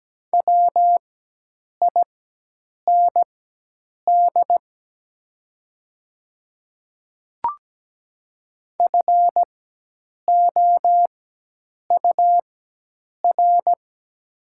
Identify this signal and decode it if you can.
Morse code, heard as a keyed tone, about 17 wpm, spaced as for 8 wpm, 700 Hz. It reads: WIND FOUR
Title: 4x Character Spacing - Top 500 Words - 17wpm